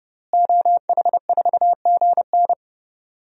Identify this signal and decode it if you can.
Morse code, heard as a keyed tone, about 32 wpm, 700 Hz. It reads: OH4GN
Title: DXCC-100 Call Signs - 5 Characters - Repeated 3x - 30wpm